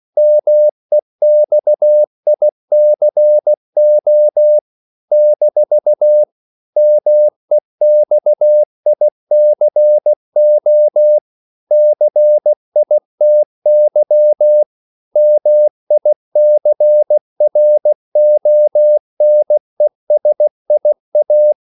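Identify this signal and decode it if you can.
Morse code, heard as a keyed tone, about 16 wpm, 600 Hz. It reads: MEXICO - MEXICO CITY MICRONESIA
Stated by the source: CW practice of national capitals at 16 words per minute, from ON6ZQ.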